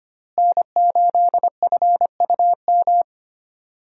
Morse code, heard as a keyed tone, about 25 wpm, 700 Hz. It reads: N8FUM
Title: USA Call Signs - 5 Characters - Repeated 2x - 25wpm